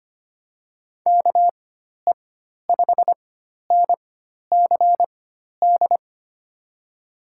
Morse code, heard as a keyed tone, about 25 wpm, spaced as for 12 wpm, 700 Hz. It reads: KE5NCD
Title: USA Call Signs - 6 Characters - 4x Character Spacing - 25wpm